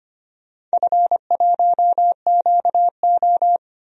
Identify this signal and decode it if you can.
Morse code, heard as a keyed tone, about 25 wpm, 690 Hz. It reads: F1QO
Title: DXCC-100 Call Signs - 4 Characters - 25wpm